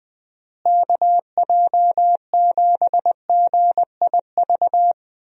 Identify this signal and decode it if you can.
Morse code, heard as a keyed tone, about 21 wpm, 700 Hz. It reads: KJ7GIV
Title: USA Call Signs - 6 Characters - Repeated 2x - 20wpm